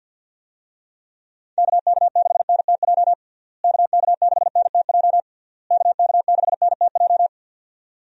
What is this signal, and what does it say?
Morse code, heard as a keyed tone, about 50 wpm, 700 Hz. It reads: KK6NTJ KK6NTJ KK6NTJ
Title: USA Call Signs - 6 Characters - Repeated 3x - 50wpm